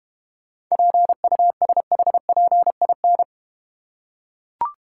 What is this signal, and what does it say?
Morse code, heard as a keyed tone, about 31 wpm, 700 Hz. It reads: PUSHPIN
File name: Easy Words - Speed-Racing with Hints - 25wpm